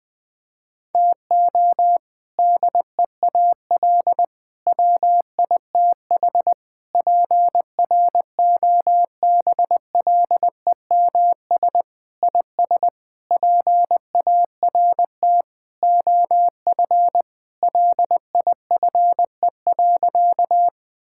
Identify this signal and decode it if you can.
Morse code, heard as a keyed tone, about 21 wpm, 700 Hz. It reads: TO DEAL WITH PROBLEMS IS PART OF LIFE.
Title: Sentences with 3-Word Phrases - Spoken Beginning - 20wpm